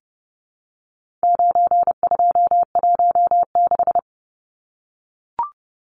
Morse code, heard as a keyed tone, about 30 wpm, 700 Hz. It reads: 9216
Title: Numbers - 4-Digit 30wpm